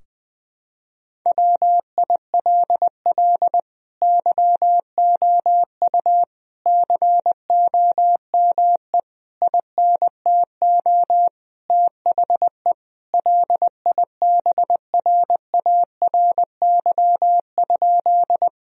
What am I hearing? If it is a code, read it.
Morse code, heard as a keyed tone, about 20 wpm, 700 Hz. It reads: WILL YOU COME INTO THE LIBRARY?
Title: Sentences from Top 1000 Words - Review Part 4 - 20wpm